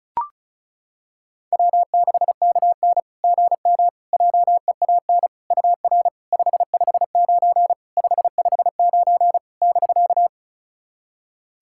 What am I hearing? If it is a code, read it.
Morse code, heard as a keyed tone, about 36 wpm, 690 Hz. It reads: W6KN GM JEAN UR 559 559 <BK>
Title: SOTA QSO Practice 35wpm